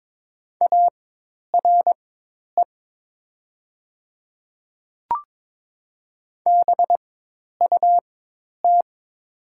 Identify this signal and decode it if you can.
Morse code, heard as a keyed tone, about 22 wpm, spaced as for 10 wpm, 700 Hz. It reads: ARE BUT
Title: 4x Character Spacing - Top 100 Words - 22wpm